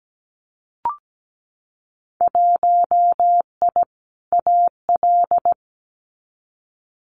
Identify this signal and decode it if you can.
Morse code, heard as a keyed tone, about 17 wpm, 700 Hz. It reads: 1I AL
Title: Winter Field Day Exchanges 17wpm